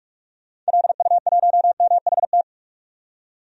Morse code, heard as a keyed tone, about 46 wpm, 700 Hz. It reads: LU1MHT
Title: DXCC-100 Call Signs - 6 Characters - 45wpm